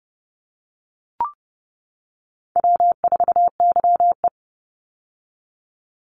Morse code, heard as a keyed tone, about 30 wpm, 700 Hz. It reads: W4YE
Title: Easy Call Signs - 30wpm